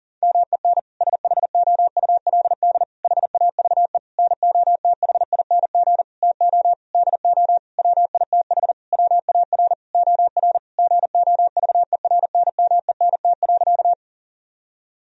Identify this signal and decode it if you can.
Morse code, heard as a keyed tone, about 40 wpm, 700 Hz. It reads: MEN SHOULD HAVE NOTHING TO DO WITH WAR OR GOVERNMENT.
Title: Sentences from Top 1000 Words - Review Part 3 - 40wpm